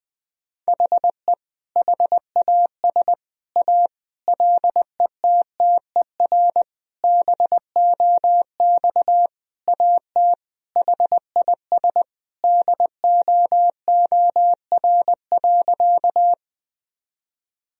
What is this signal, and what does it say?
Morse code, heard as a keyed tone, about 20 wpm, 700 Hz. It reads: HE HAS A LETTER BOX AT HIS DOOR.
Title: Sentences from Top 1000 Words - Review Part 2 - 20wpm